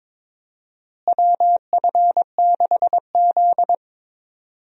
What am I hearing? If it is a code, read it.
Morse code, heard as a keyed tone, about 22 wpm, 700 Hz. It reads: WF6Z
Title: USA Call Signs - 4 Characters - Repeated 3x - 22wpm